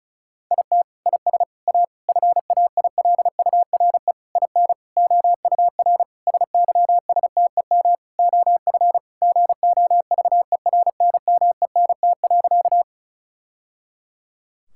Morse code, heard as a keyed tone, about 36 wpm, 700 Hz. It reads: IT IS A FAILURE IN OUR SYSTEM OF GOVERNMENT.
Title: Sentences from Top 1000 Words - Review Part 6 - 35wpm